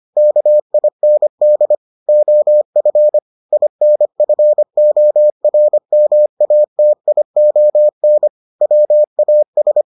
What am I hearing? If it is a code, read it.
Morse code, heard as a keyed tone, about 25 wpm, 600 Hz. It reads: KIND OF INFORMATION WAS